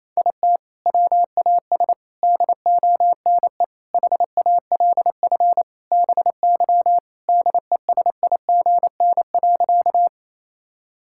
Morse code, heard as a keyed tone, about 27 wpm, 700 Hz. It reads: IT WAS DONE HALF BY DESIGN.
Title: Sentences from Top 1000 Words - Review Part 5 - 28wpm